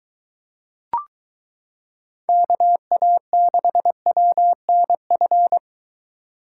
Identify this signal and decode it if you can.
Morse code, heard as a keyed tone, about 23 wpm, 700 Hz. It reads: KA6WNF